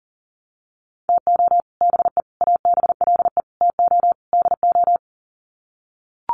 Morse code, heard as a keyed tone, about 40 wpm, 700 Hz. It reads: TO BE ABLE TO DO
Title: Sets of 5 Words 40wpm